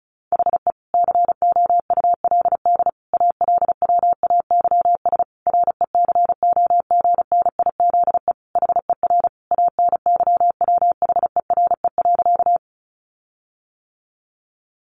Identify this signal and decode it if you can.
Morse code, heard as a keyed tone, about 34 wpm, 700 Hz. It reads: HE COULD ALWAYS RECOGNIZE HER ANYWHERE.
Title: Sentences from Top 1400 Words 35wpm